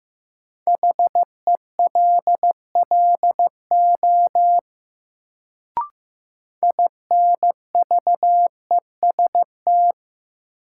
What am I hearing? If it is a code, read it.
Morse code, heard as a keyed tone, about 15 wpm, 700 Hz. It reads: HELLO INVEST
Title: Sentences from Top 1100 Words - New Word Drill Alt - 15wpm